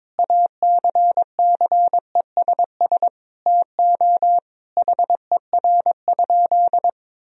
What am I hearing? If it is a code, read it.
Morse code, heard as a keyed tone, about 22 wpm, 700 Hz. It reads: ACCESS TO HER?
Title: Sentences from Top 1000 Words - Review Part 4 - 22wpm